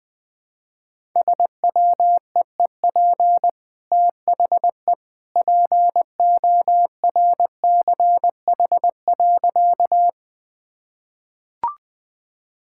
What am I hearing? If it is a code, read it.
Morse code, heard as a keyed tone, about 20 wpm, 700 Hz. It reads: SWEEP THE PORCH.
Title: Easy Sentences with Hints - 20wpm